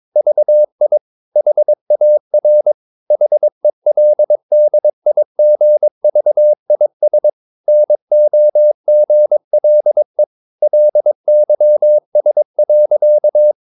Morse code, heard as a keyed tone, about 22 wpm, 600 Hz. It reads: VI HAR HELDIGVIS NOGLE LYS.
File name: Ferie QRS 21wpm